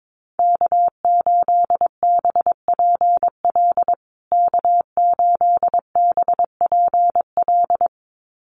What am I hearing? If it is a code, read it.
Morse code, heard as a keyed tone, about 22 wpm, 700 Hz. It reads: K8BPL K8BPL
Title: USA Call Signs Repeated 2x 22wpm